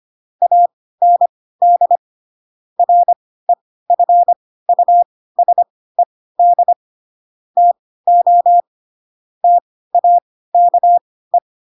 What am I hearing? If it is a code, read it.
Morse code, heard as a keyed tone, about 25 wpm, spaced as for 16 wpm, 700 Hz. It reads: AND REFUSED TO TAKE
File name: Pinocchio / Pinocchio ch3 10 25 ik1hge